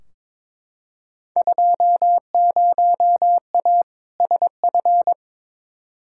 Morse code, heard as a keyed tone, about 22 wpm, 700 Hz. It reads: 20A SF
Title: Field Day Exchanges Encore - 22wpm